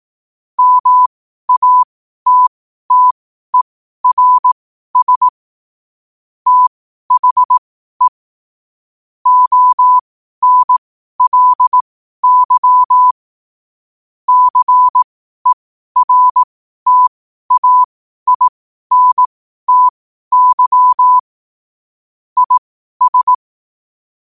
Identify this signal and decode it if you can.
Morse code, heard as a keyed tone, about 18 wpm, spaced as for 12 wpm, 1000 Hz. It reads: MATTERS THE ONLY CERTAINTY IS
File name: qotdfiles / 12wpm